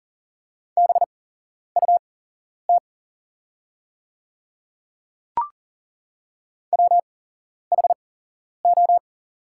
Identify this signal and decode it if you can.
Morse code, heard as a keyed tone, about 41 wpm, spaced as for 11 wpm, 700 Hz. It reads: BUT WHO
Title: 8x Character Spacing - Top 200 Words - 40wpm